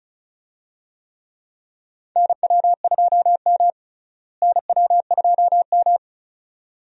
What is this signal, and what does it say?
Morse code, heard as a keyed tone, about 36 wpm, 700 Hz. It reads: NW2M NW2M
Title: USA Call Signs - 4 Characters - Repeated 2x - 35wpm